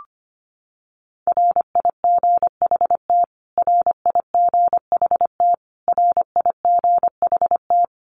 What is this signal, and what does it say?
Morse code, heard as a keyed tone, about 25 wpm, 700 Hz. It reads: RIGHT RIGHT RIGHT
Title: Top 300 Words Repeated 25wpm